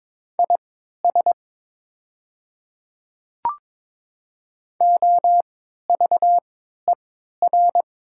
Morse code, heard as a keyed tone, about 22 wpm, spaced as for 12 wpm, 700 Hz. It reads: IS OVER
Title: 3x Character Spacing - Top 100 Words - 22wpm